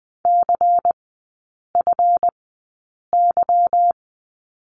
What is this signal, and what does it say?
Morse code, heard as a keyed tone, about 20 wpm, spaced as for 8 wpm, 700 Hz. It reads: CFY
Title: 184 Lesson 30e - Letter b - Three Characters 20wpm